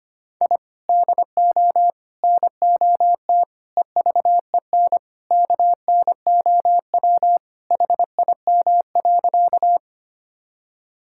Morse code, heard as a keyed tone, about 25 wpm, 700 Hz. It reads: I DO NOT EVEN KNOW HIM.
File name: practice / 25wpm / Sentences with 4-Word Phrases - 25wpm